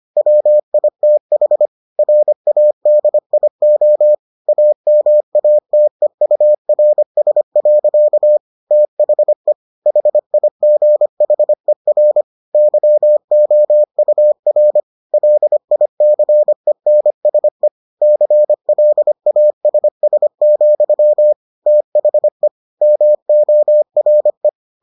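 Morse code, heard as a keyed tone, about 25 wpm, 600 Hz. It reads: WITH RADIO AMATEURS. THE HIGHER YOUR LICENSE CLASS, THE MORE
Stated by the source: Morse edition of the Foundations of Amateur Radio episodes.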